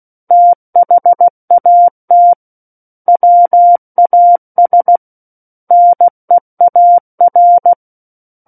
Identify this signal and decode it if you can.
Morse code, heard as a keyed tone, about 16 wpm, 700 Hz. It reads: THAT WAS NEAR
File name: Book One / 07-Chapter 7 - 16 WPM